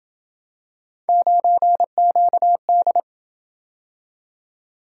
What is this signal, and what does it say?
Morse code, heard as a keyed tone, about 26 wpm, 700 Hz. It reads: 9QD